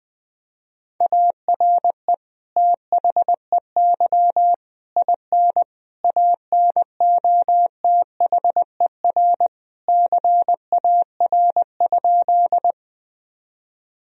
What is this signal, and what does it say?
Morse code, heard as a keyed tone, about 20 wpm, 700 Hz. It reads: ARE THEY IN ANOTHER CAR?
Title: Sentences from Top 1000 Words - Review Part 1 - 20wpm